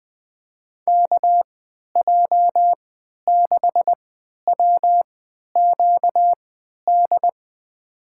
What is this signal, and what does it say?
Morse code, heard as a keyed tone, about 20 wpm, spaced as for 11 wpm, 700 Hz. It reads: KJ6WQD
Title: USA Call Signs - 3x Character Spacing - 20wpm